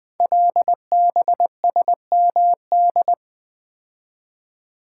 Morse code, heard as a keyed tone, about 20 wpm, 700 Hz. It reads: LBSMD
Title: Code Groups - 5 Letters - 20wpm